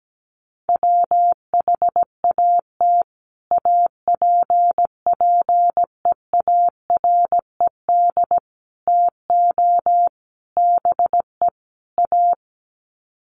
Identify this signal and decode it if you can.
Morse code, heard as a keyed tone, about 17 wpm, 700 Hz. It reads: WHAT APPEARED TO BE A
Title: Sets of 5 Words Encore2 17wpm